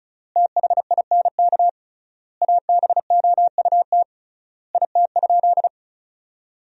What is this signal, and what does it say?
Morse code, heard as a keyed tone, about 36 wpm, 700 Hz. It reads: THINK ABOUT IT?
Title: 3x Word-Spacing - Sentences from Top 200 Words - 35wpm